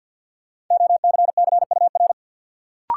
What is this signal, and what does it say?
Morse code, heard as a keyed tone, about 53 wpm, 700 Hz. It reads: OCCUR
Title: Sentences from Top 900 Words - New Word Drill - 50wpm